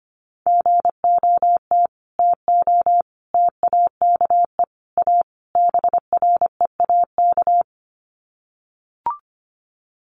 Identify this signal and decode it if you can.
Morse code, heard as a keyed tone, about 25 wpm, 700 Hz. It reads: GOT TO TAKE A BREAK
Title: Sets of 5 Words Encore 25wpm